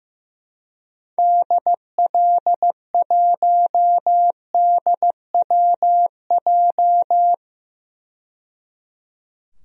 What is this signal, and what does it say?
Morse code, heard as a keyed tone, about 15 wpm, 700 Hz. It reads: DL1DWJ
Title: DXCC-100 Call Signs - 6 Characters - 15wpm